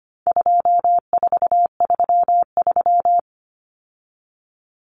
Morse code, heard as a keyed tone, about 25 wpm, 700 Hz. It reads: 2433
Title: Common Cut Numbers - 4-Digit 25wpm